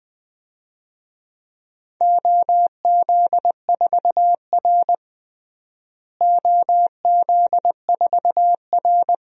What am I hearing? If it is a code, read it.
Morse code, heard as a keyed tone, about 20 wpm, 700 Hz. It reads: OZ4R OZ4R
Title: DXCC-100 Call Signs - 4 Characters - Repeated 2x - 20wpm